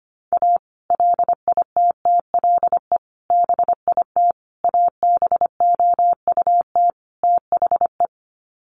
Morse code, heard as a keyed tone, about 25 wpm, 700 Hz. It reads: A LITTLE BIT ABOUT THE
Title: Sets of 5 Words Encore2 25wpm